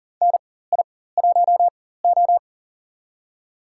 Morse code, heard as a keyed tone, about 40 wpm, spaced as for 19 wpm, 700 Hz. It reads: NI1O
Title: USA Call Signs - 4 Characters - 4x Character Spacing - 40wpm